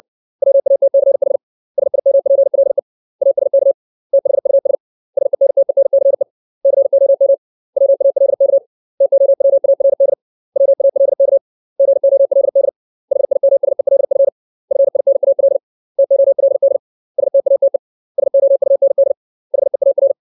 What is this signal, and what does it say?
Morse code, heard as a keyed tone, about 60 wpm, 550 Hz. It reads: WATCH SIMPLE ASK THUS SENTENCE COM WALK TOWARD WALK GOLD HIMSELF FINAL TOLD STATE SOUND HEAR
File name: 632 words / 632 Words 60WPM